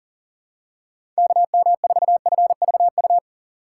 Morse code, heard as a keyed tone, about 39 wpm, 700 Hz. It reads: KM4FVU
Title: USA Call Signs - 6 Characters - 3x Character Spacing - 40wpm